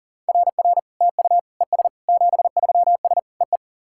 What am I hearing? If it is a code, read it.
Morse code, heard as a keyed tone, about 38 wpm, 690 Hz. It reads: RR TU ES 73S EE
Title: SOTA QSO Practice 40wpm